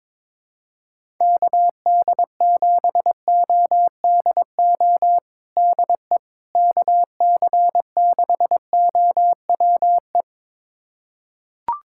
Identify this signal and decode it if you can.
Morse code, heard as a keyed tone, about 22 wpm, 700 Hz. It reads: KD7ODO DE KC6OWE
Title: USA Callsign DE Callsign - 6 Characters - 22wpm